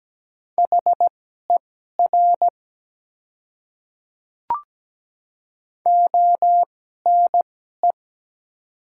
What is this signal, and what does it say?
Morse code, heard as a keyed tone, about 17 wpm, spaced as for 12 wpm, 700 Hz. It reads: HER ONE
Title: 2x Character Spacing - Top 200 Words - 17wpm